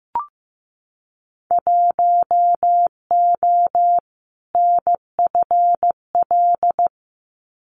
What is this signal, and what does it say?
Morse code, heard as a keyed tone, about 15 wpm, 700 Hz. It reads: 1O NFL
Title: Winter Field Day Exchanges 15wpm